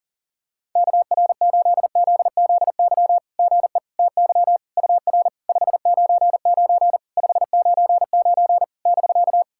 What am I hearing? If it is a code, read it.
Morse code, heard as a keyed tone, about 40 wpm, 690 Hz. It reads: KR8ZZY GE TY UR 599 599 <BK>
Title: SOTA QSO Practice 40wpm